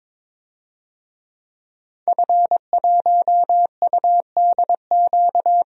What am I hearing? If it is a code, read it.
Morse code, heard as a keyed tone, about 21 wpm, 700 Hz. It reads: F1UDQ
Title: DXCC-100 Call Signs - 5 Characters - 22wpm